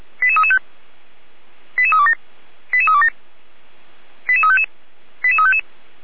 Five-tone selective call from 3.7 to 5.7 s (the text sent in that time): 88355 88355